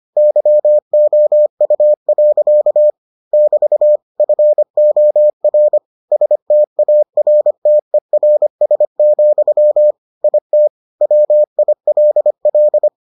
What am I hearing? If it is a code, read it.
Morse code, heard as a keyed tone, about 25 wpm, 600 Hz. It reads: YOU. <BT> FOR STARTERS, IT WILL